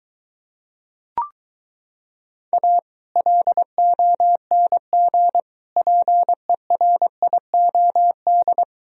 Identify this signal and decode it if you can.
Morse code, heard as a keyed tone, about 23 wpm, 700 Hz. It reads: A LONG PERIOD